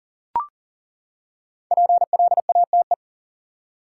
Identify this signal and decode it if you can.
Morse code, heard as a keyed tone, about 40 wpm, 700 Hz. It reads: PLATE
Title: Five Letter Words - 40wpm